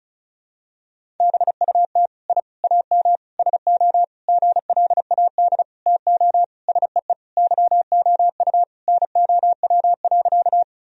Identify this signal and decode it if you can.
Morse code, heard as a keyed tone, about 36 wpm, 700 Hz. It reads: BUT I AM SO GLAD TO SEE YOU NOW.
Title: Sentences from Top 1000 Words - Review Part 2 - 35wpm